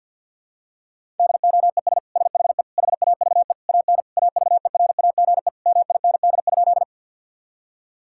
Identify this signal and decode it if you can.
Morse code, heard as a keyed tone, about 50 wpm, 700 Hz. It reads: DOES SHE HAVE AN AVERAGE MIND?
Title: Sentences from Top 1000 Words - Review Part 6 - 50wpm